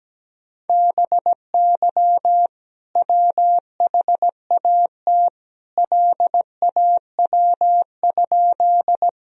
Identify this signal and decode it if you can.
Morse code, heard as a keyed tone, about 17 wpm, 700 Hz. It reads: BY WHAT LAW?
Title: Sentences from Top 1000 Words - Review Part 3 - 17wpm